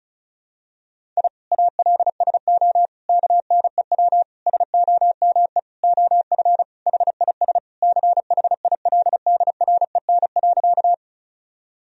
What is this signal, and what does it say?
Morse code, heard as a keyed tone, about 34 wpm, 700 Hz. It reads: I ALSO KNEW SOME OF HIS CHILDREN.
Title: Sentences from Top 1000 Words - Review Part 1 - 35wpm